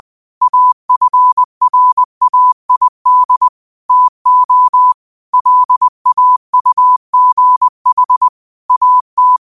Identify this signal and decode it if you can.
Morse code, heard as a keyed tone, about 20 wpm, 1000 Hz. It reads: AFRAID TO LAUGH AT